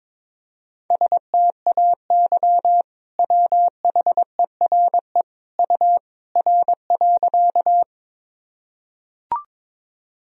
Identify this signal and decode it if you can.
Morse code, heard as a keyed tone, about 22 wpm, 700 Hz. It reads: STAY WHERE U R.
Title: Sentences from Top 500 Words and Abbreviations 22wpm